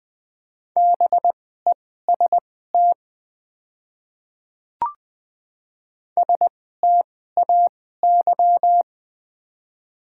Morse code, heard as a keyed tone, about 20 wpm, spaced as for 14 wpm, 700 Hz. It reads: BEST STAY
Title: 2x Character Spacing - Top 500 Words - 20wpm